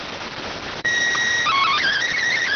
SSTV (slow-scan television), PD 290.